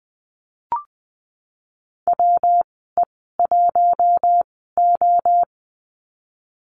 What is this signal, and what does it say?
Morse code, heard as a keyed tone, about 20 wpm, spaced as for 14 wpm, 700 Hz. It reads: WE1O
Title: 140 Lesson 23g - Letter f - Call Signs 20wpm